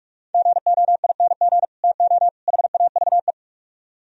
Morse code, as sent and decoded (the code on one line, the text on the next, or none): --. --- .. -. --. / - --- / .... .- ...- .
GOING TO HAVE